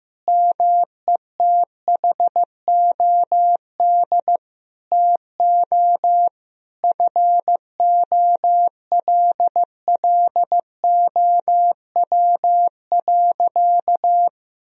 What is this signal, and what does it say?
Morse code, heard as a keyed tone, about 15 wpm, 700 Hz. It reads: METHOD TO FOLLOW.
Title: Sentences from Top 1000 Words - Review Part 3 - 15wpm